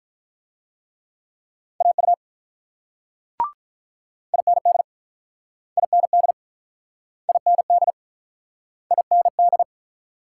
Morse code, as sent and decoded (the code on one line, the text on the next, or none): .- ..- / .. -. -.. / .. -. -.. / .. -. -.. / .. -. -..
AU IND IND IND IND